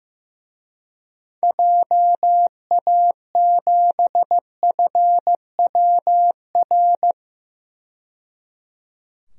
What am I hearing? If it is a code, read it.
Morse code, heard as a keyed tone, about 15 wpm, 700 Hz. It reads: JA7FWR
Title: DXCC-100 Call Signs - 6 Characters - 15wpm